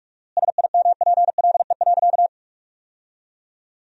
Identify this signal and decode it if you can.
Morse code, heard as a keyed tone, about 46 wpm, 700 Hz. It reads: SIMPLE.
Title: 4x Word-Spacing - Sentences from Top 500 Words - 45wpm